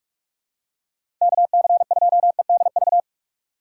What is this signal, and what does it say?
Morse code, heard as a keyed tone, about 43 wpm, 700 Hz. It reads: KC2EDV
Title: USA Call Signs - 6 Characters - 4x Character Spacing - 45wpm